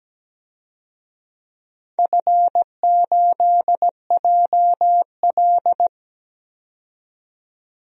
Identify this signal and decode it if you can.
Morse code, heard as a keyed tone, about 17 wpm, 700 Hz. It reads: F8JL
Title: DXCC-100 Call Signs - 4 Characters - 17wpm